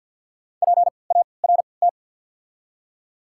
Morse code, heard as a keyed tone, about 53 wpm, spaced as for 29 wpm, 690 Hz. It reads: PART
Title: 3x Character Spacing - Top 300 Words - 50wpm